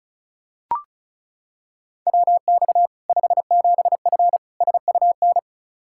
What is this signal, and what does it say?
Morse code, heard as a keyed tone, about 36 wpm, 700 Hz. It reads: WX 57F SUN
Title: K1USN SST QSO Practice 35wpm